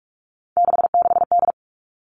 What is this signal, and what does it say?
Morse code, heard as a keyed tone, about 46 wpm, 700 Hz. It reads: 66D
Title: Three Challenging Letter-Number Combinations 45wpm